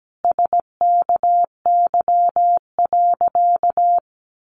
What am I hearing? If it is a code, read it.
Morse code, heard as a keyed tone, about 17 wpm, 700 Hz. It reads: SKY.
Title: Additional Easy Phrases 17wpm